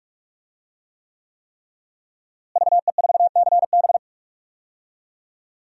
Morse code, heard as a keyed tone, about 46 wpm, 700 Hz. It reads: VE4CB
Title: DXCC-100 Call Signs - 5 Characters - Repeated 3x - 45wpm